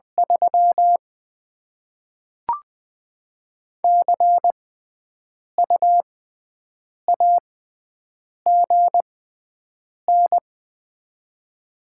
Morse code, heard as a keyed tone, about 21 wpm, spaced as for 7 wpm, 700 Hz. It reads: 3 CU AGN
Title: Top 20 QSO Elements - 6x Character Spacing - 20wpm